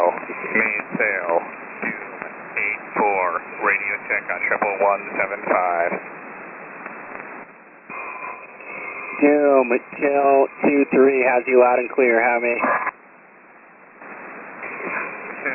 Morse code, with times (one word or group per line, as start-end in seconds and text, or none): none